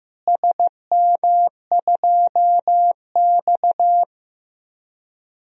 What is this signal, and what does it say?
Morse code, heard as a keyed tone, about 15 wpm, 700 Hz. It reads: SM2X
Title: DXCC-100 Call Signs - 4 Characters - Repeated 3x - 15wpm